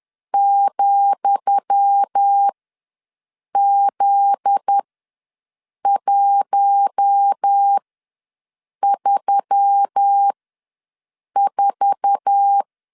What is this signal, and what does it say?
Morse code, heard as a keyed tone, about 10 wpm, spaced as for 6 wpm, 800 Hz. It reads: ,Z134
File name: LEZIONI / 10 SIGNS / 10D COMA ws 10wpm 4314